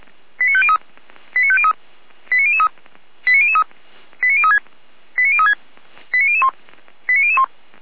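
Five-tone selective call (ZVEI), heard as quick runs of five tones, tones of about 70 ms. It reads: 89503 89503 89003 89003 89036 89036 89001 89001
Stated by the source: ZVEI five-tone selective call received by SDR.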